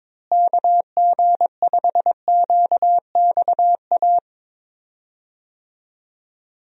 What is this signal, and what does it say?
Morse code, heard as a keyed tone, about 22 wpm, 700 Hz. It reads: KG5QXA
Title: USA Call Signs - 6 Characters - Repeated 3x - 22wpm